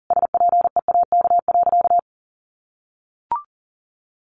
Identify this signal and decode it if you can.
Morse code, heard as a keyed tone, about 37 wpm, 700 Hz. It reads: SPEAK.
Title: Sentences from Top 1500 Words 40wpm